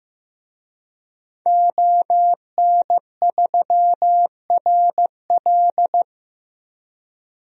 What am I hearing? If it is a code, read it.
Morse code, heard as a keyed tone, about 15 wpm, 700 Hz. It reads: ON3RL
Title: DXCC-100 Call Signs - 5 Characters - Repeated 3x - 15wpm